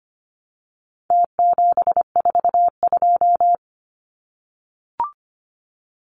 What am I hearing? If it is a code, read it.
Morse code, heard as a keyed tone, about 25 wpm, 700 Hz. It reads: T742
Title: Numbers - Contest Style - 4-Digit 25wpm